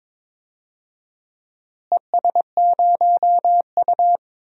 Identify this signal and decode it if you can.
Morse code, heard as a keyed tone, about 21 wpm, 700 Hz. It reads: ES0U
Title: DXCC-100 Call Signs - 4 Characters - Repeated 3x - 22wpm